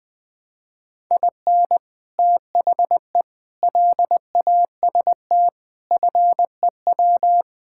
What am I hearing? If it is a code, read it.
Morse code, heard as a keyed tone, about 20 wpm, 700 Hz. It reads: IN THE LAST FEW
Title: Sentences with 3-Word Phrases - 20wpm